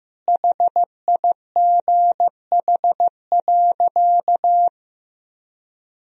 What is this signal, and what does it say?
Morse code, heard as a keyed tone, about 15 wpm, 700 Hz. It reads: HIGH.
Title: Sentences from Top 1000 Words - Review Part 4 - 15wpm